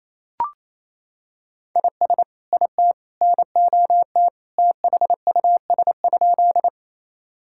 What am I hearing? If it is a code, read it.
Morse code, heard as a keyed tone, about 28 wpm, 700 Hz. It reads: IS IT NOT THUS?
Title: Sentences from Top 1000 Words - Review Part 5 - 28wpm